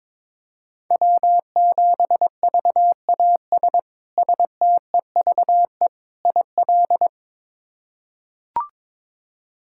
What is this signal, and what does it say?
Morse code, heard as a keyed tone, about 21 wpm, 700 Hz. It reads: W7VAS STEVE IL
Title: K1USN SST QSO Practice - 22wpm